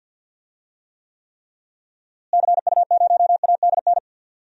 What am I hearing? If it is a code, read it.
Morse code, heard as a keyed tone, about 52 wpm, 700 Hz. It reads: CU0ADN
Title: DXCC-100 Call Signs - 6 Characters - Repeated 3x - 50wpm